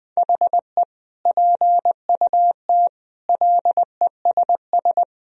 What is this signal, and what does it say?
Morse code, heard as a keyed tone, about 20 wpm, 700 Hz. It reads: HE PUT LESS
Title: Sentences from Top 1000 Words - Review Part 3 - 20wpm